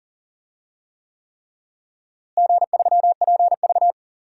Morse code, heard as a keyed tone, about 40 wpm, 700 Hz. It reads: G3PV
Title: DXCC-100 Call Signs - 4 Characters - Repeated 3x - 40wpm